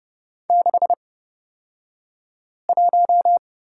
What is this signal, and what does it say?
Morse code, heard as a keyed tone, about 30 wpm, spaced as for 5 wpm, 690 Hz. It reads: 61
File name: Lesson 36d - BT - Two Characters 30wpm